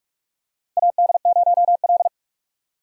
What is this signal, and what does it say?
Morse code, heard as a keyed tone, about 46 wpm, 700 Hz. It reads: AD0L
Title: USA Call Signs - 4 Characters - 2x Character Spacing - 45wpm